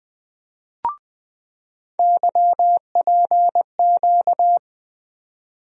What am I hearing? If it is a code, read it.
Morse code, heard as a keyed tone, about 21 wpm, 700 Hz. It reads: YPQ
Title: USA Call Sign Postfixes - 3 Letters 20wpm